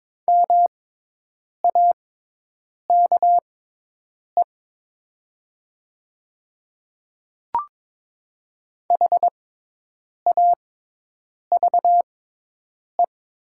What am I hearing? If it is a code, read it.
Morse code, heard as a keyed tone, about 22 wpm, spaced as for 8 wpm, 700 Hz. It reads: MAKE HAVE
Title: 6x Character Spacing - Top 200 Words - 22wpm